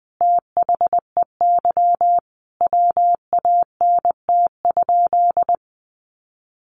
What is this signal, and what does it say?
Morse code, heard as a keyed tone, about 20 wpm, 700 Hz. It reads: THEY WANT?
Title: Sentences from Top 200 Words 20wpm